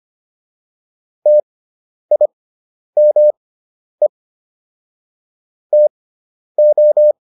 Morse code, heard as a keyed tone, about 25 wpm, spaced as for 10 wpm, 600 Hz. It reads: TIME TO